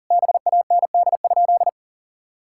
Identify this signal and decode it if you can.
Morse code, heard as a keyed tone, about 40 wpm, 700 Hz. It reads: BAND?
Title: Sentences from Top 1600 Words 40wpm